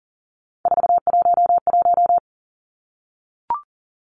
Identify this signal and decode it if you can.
Morse code, heard as a keyed tone, about 40 wpm, 700 Hz. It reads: 411
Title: Numbers - 3-Digit 40wpm